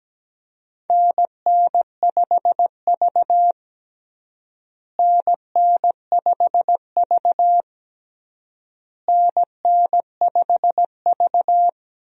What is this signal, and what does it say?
Morse code, heard as a keyed tone, about 17 wpm, 700 Hz. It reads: NN5V NN5V NN5V
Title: USA Call Signs - 4 Characters - Repeated 3x - 17wpm